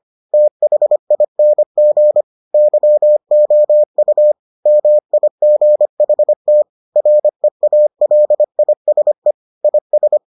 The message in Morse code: THING YOU MIGHT REALISE IS

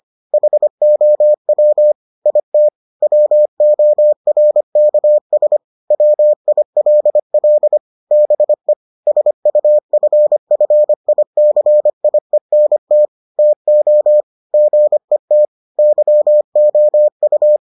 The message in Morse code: HOW IT WORKS WILL BE SUFFICIENT TO GET YOU

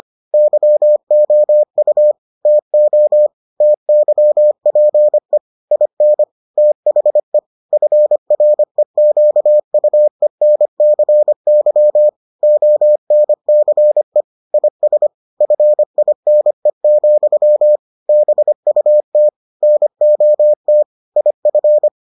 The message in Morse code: YOU TO TYPE IN THE FREQUENCY ONCE IS FINE, BUT NOT IF

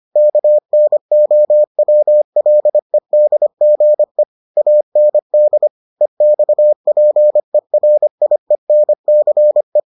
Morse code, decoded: KNOWLEDGE AND EXPERIENCE

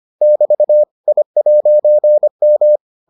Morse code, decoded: <BT> I'M

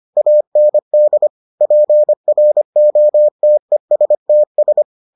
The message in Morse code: AND PROTESTS

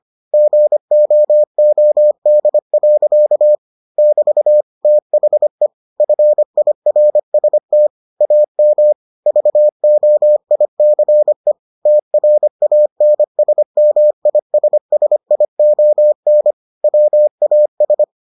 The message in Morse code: GOOD. <BT> THE FIRST AM VOICE TRANSMISSION WAS

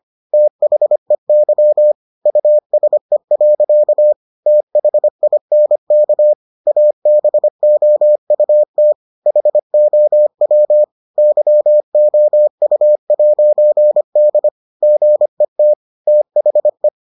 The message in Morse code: THEY USE. THINK ABOUT HOW YOU'D GET THE

